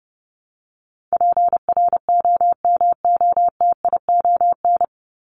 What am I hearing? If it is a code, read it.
Morse code, heard as a keyed tone, about 30 wpm, 700 Hz. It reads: PROMOTION
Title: Top 1501-1750 Words 30wpm